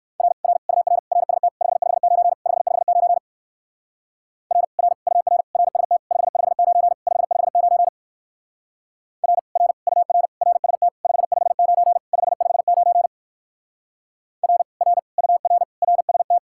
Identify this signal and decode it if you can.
Morse code, heard as a keyed tone, about 54 wpm, 700 Hz. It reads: R R UR RST 559 559 R R UR RST 559 559 R R UR RST 559 559 R R UR RST
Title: Speed-Racing Common QSO Phrases - 45wpm